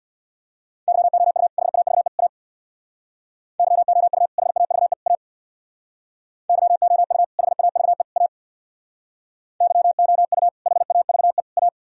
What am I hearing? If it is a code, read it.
Morse code, heard as a keyed tone, about 64 wpm, 700 Hz. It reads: YOU HAVE A YOU HAVE A YOU HAVE A YOU HAVE A